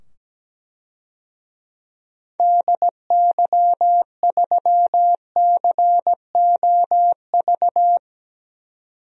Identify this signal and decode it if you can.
Morse code, heard as a keyed tone, about 17 wpm, 700 Hz. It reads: DY3COV
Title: DXCC-100 Call Signs - 6 Characters - 17wpm